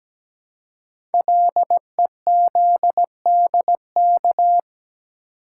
Morse code, heard as a keyed tone, about 17 wpm, 700 Hz. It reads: LEZDK
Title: Code Groups - 5 Letters - 17wpm